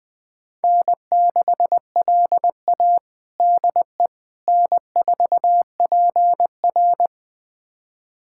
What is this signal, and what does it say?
Morse code, heard as a keyed tone, about 20 wpm, 700 Hz. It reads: N6LA DE N4PR
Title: USA Callsign DE Callsign - 4 Characters - 20wpm